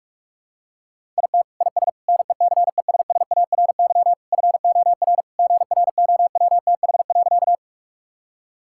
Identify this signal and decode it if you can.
Morse code, heard as a keyed tone, about 46 wpm, 700 Hz. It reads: IT IS NECESSARY FOR GROWTH.
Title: Sentences from Top 1000 Words - Review Part 5 - 45wpm